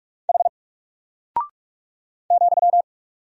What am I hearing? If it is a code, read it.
Morse code, heard as a keyed tone, about 44 wpm, 700 Hz. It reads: H ,
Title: Single Character (Letter:Number:Punctuation:Prosign) Alt 45wpm